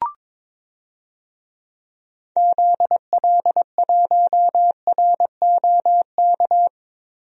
Morse code, heard as a keyed tone, about 22 wpm, 700 Hz. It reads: ZL1ROK
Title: DXCC-100 Call Signs - 6 Characters - Repeated 3x - 22wpm